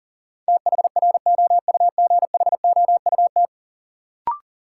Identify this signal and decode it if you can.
Morse code, heard as a keyed tone, about 40 wpm, 700 Hz. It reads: THROUGHOUT
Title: Sentences from Top 800 Words - New Word Drill - 40wpm